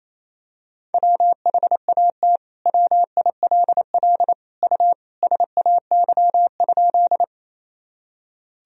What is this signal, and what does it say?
Morse code, heard as a keyed tone, about 28 wpm, 700 Hz. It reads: WHAT WILL U SAY?
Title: Sentences from Top 200 Words and Abbreviations 28wpm